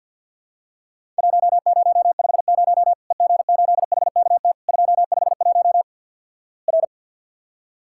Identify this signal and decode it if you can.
Morse code, heard as a keyed tone, about 48 wpm, 700 Hz. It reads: 1050 EG8HKT 251 R
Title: Worked All Europe DX Contest - QTC Exchange - 50wpm